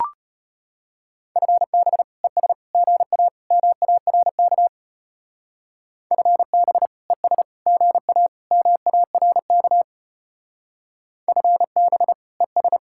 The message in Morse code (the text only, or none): FB ES GA MARK FB ES GA MARK FB ES